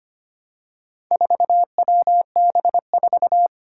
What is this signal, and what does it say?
Morse code, heard as a keyed tone, about 25 wpm, 700 Hz. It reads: 4WB4
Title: Four Challenging Letter-Number Combinations 25wpm